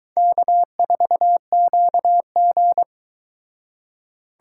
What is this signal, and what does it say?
Morse code, heard as a keyed tone, about 23 wpm, 700 Hz. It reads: K4QG